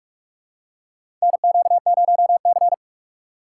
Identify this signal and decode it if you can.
Morse code, heard as a keyed tone, about 46 wpm, 700 Hz. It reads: NQ0C
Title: USA Call Signs - 3x Character Spacing - 45wpm